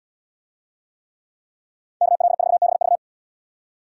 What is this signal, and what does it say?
Morse code, heard as a keyed tone, about 78 wpm, 690 Hz. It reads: KL4BV